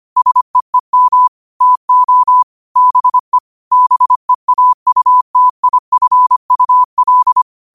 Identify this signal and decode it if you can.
Morse code, heard as a keyed tone, about 25 wpm, 1000 Hz. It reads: SEEM TO BE BEAUTIFUL
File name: qotdfiles / 25wpm